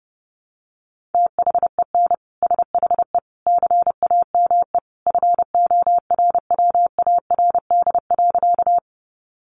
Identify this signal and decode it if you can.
Morse code, heard as a keyed tone, about 30 wpm, 700 Hz. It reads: THEN SHE CAME FORWARD.
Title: Sentences from Top 1300 Words 30wpm